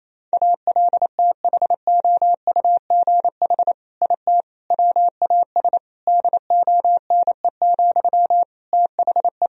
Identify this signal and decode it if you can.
Morse code, heard as a keyed tone, about 27 wpm, 700 Hz. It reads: ALTHOUGH IT WAS DONE, THE
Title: Sentences from Top 1000 Words - Review Part 3 - 28wpm